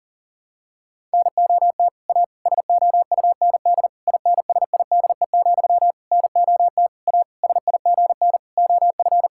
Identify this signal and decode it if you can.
Morse code, heard as a keyed tone, about 40 wpm, 700 Hz. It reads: NOT A SOUND INSIDE, NOT A SIGN OF
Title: Sentences from Top 1000 Words - Review Part 6 - 40wpm